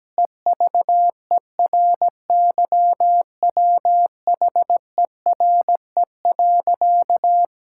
Morse code, heard as a keyed tone, about 17 wpm, 700 Hz. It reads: EVERYWHERE.
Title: Sentences from Top 1600 Words 17wpm